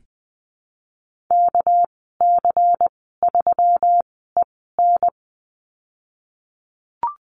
Morse code, heard as a keyed tone, about 20 wpm, spaced as for 15 wpm, 700 Hz. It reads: KC3EN
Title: 200 Lesson 32g - Letter k - Call Signs 20wpm